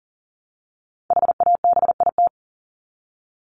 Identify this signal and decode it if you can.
Morse code, heard as a keyed tone, about 40 wpm, 700 Hz. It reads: HABIT
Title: Top 1751-2000 Words 40wpm